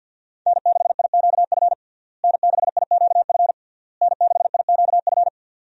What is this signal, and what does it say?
Morse code, heard as a keyed tone, about 45 wpm, 700 Hz. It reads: N6IQF N6IQF N6IQF
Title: USA Call Signs - 5 Characters - Repeated 3x - 50wpm